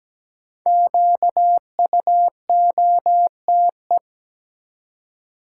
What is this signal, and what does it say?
Morse code, heard as a keyed tone, about 17 wpm, 700 Hz. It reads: QUOTE
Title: Five Letter Words - 17wpm